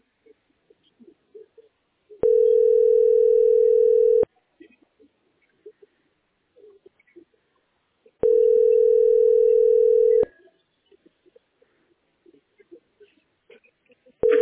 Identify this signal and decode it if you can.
Ringback tone (North America), 440 + 480 Hz.